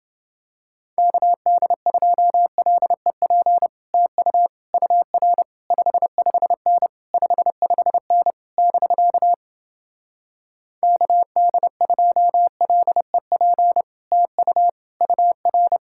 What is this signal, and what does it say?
Morse code, heard as a keyed tone, about 30 wpm, 700 Hz. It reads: KD2LEP TU UR 55N 55N <BK> KD2LEP TU UR